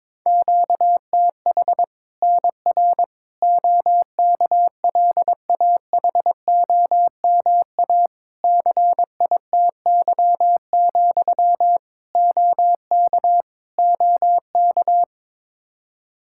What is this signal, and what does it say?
Morse code, heard as a keyed tone, about 22 wpm, 700 Hz. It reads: QTH NR OKLAHOMA CITY, OK OK
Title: Practice QSO - Exchange 1 - Beginning - 22wpm